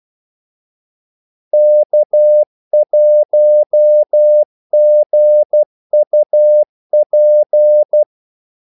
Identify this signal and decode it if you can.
Morse code, heard as a keyed tone, about 12 wpm, 600 Hz. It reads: K1GUP
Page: 2 ekstra ordmellemrum mellem kaldesignalerne